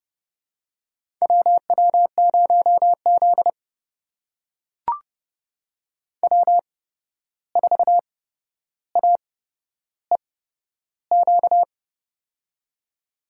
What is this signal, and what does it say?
Morse code, heard as a keyed tone, about 29 wpm, spaced as for 12 wpm, 700 Hz. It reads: WW0Z W4AEQ
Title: USA Call Signs - 8x Character Spacing - 30wpm